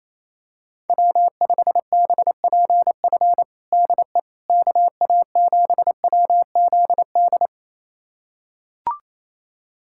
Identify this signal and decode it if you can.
Morse code, heard as a keyed tone, about 28 wpm, 700 Hz. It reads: W5BPF DE KA7WZD
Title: USA Callsign DE Callsign 28wpm